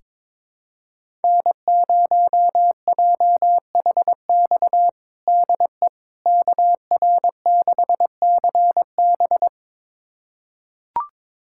Morse code, heard as a keyed tone, about 22 wpm, 700 Hz. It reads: N0JHX DE KR6CB
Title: USA Callsign DE Callsign - 5 Characters - 22wpm